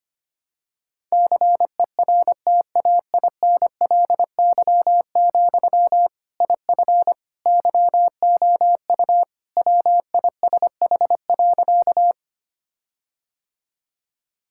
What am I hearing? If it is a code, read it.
Morse code, heard as a keyed tone, about 25 wpm, 700 Hz. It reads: CERTAINLY, IF YOU WISH.
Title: Sentences from Top 1600 Words 25wpm